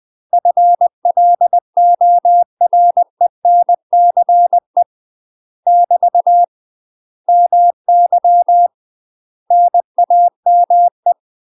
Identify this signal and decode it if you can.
Morse code, heard as a keyed tone, about 20 wpm, 700 Hz. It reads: FLORENCE <BT> MY NAME
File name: UK MASTER QSOs TRAINING / 20WPM / UK MASTER QSOs 0045